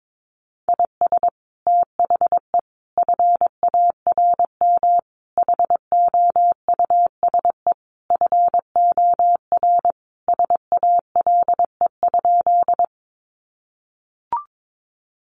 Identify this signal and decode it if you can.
Morse code, heard as a keyed tone, about 22 wpm, 700 Hz. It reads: IS THE FARM HOUSE FOR SALE?
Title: Sentences from Top 1400 Words 22wpm